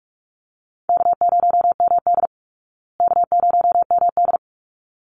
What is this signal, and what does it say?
Morse code, heard as a keyed tone, about 44 wpm, 700 Hz. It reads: K0MD K0MD
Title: Easy Call Signs Repeated 2x - 45wpm